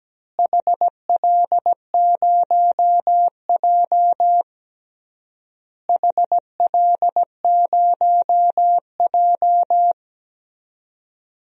Morse code, heard as a keyed tone, about 17 wpm, 700 Hz. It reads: HL0J HL0J
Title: DXCC-100 Call Signs - 4 Characters - Repeated 3x - 17wpm